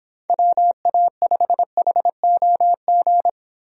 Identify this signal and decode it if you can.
Morse code, heard as a keyed tone, about 24 wpm, 700 Hz. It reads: WA5HOG